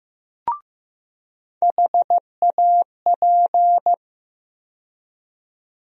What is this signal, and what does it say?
Morse code, heard as a keyed tone, about 15 wpm, 700 Hz. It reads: HAP
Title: USA Call Sign Postfixes - 3 Letters 15wpm